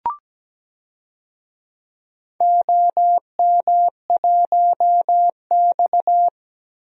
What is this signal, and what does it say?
Morse code, heard as a keyed tone, about 17 wpm, 700 Hz. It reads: OM1X
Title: DXCC-100 Call Signs - 4 Characters - 17wpm